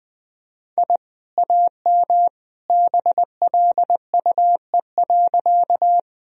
Morse code, heard as a keyed tone, about 20 wpm, 700 Hz. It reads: I AM BLUE.
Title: Sentences from Top 1000 Words - Review Part 3 - 20wpm